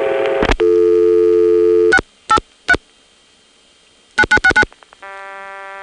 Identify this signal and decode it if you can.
DTMF, heard as telephone touch tones, tones of about 70 ms. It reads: 683 9969